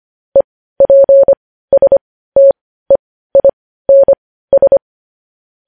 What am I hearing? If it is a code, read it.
Morse code, heard as a keyed tone, about 25 wpm, spaced as for 15 wpm, 550 Hz. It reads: EPSTEINS
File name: podcast / News headlines-1x-25x15